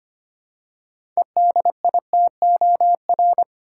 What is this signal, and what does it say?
Morse code, heard as a keyed tone, about 25 wpm, 700 Hz. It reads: EDITOR
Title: Six Letter Words - 25wpm